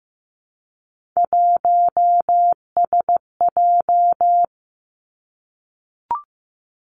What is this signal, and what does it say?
Morse code, heard as a keyed tone, about 15 wpm, 700 Hz. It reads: 1SJ
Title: Three Challenging Letter-Number Combinations 15wpm